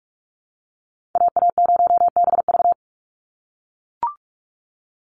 Morse code, heard as a keyed tone, about 44 wpm, 700 Hz. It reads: AA0BV
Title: NA SOTA Activator Callsigns 45wpm